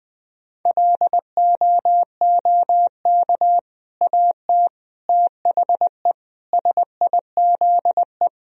Morse code, heard as a keyed tone, about 19 wpm, 700 Hz. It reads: LOOK AT THE SIZE
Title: Sentences from Top 1000 Words - Review Part 3 - 20wpm